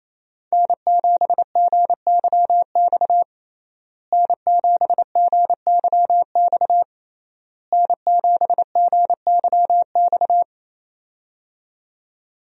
Morse code, heard as a keyed tone, about 28 wpm, 700 Hz. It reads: N7GYX N7GYX N7GYX
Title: USA Call Signs - 5 Characters - Repeated 3x - 28wpm